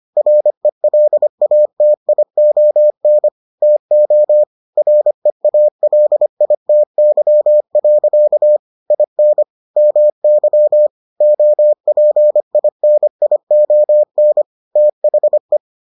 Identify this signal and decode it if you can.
Morse code, heard as a keyed tone, about 25 wpm, 600 Hz. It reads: RELATION TO REALITY. IN MY OPINION THE